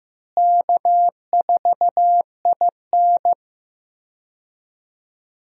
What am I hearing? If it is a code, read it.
Morse code, heard as a keyed tone, about 15 wpm, 700 Hz. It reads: K4IN
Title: USA Call Signs - 4 Characters - Repeated 3x - 15wpm